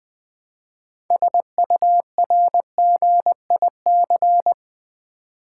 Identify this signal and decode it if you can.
Morse code, heard as a keyed tone, about 20 wpm, 700 Hz. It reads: SURGIC
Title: Build A Word - 3 Letters At A Time - 8-Letter Words - 20wpm